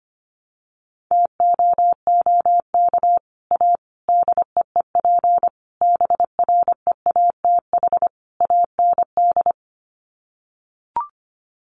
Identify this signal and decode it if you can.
Morse code, heard as a keyed tone, about 25 wpm, 700 Hz. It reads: TOOK A DEEP BREATH AND
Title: Sets of 5 Words Encore 25wpm